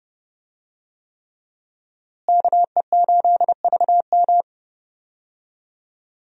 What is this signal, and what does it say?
Morse code, heard as a keyed tone, about 31 wpm, 700 Hz. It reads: KE8VM
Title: USA Call Signs - 5 Characters - Repeated 3x - 30wpm